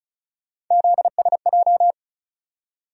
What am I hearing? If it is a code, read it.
Morse code, heard as a keyed tone, about 36 wpm, 700 Hz. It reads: ZSJ
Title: USA Call Sign Postfixes - 3 Letters 35wpm